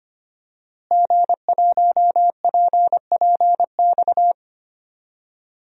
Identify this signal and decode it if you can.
Morse code, heard as a keyed tone, about 25 wpm, 700 Hz. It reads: G1PPX
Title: DXCC-100 Call Signs - 5 Characters - Repeated 2x - 25wpm